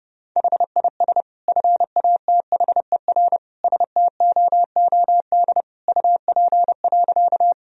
Morse code, heard as a keyed tone, about 30 wpm, 700 Hz. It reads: HIS FATHER STOOD UP.
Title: Sentences from Top 1000 Words - Review Part 2 - 30wpm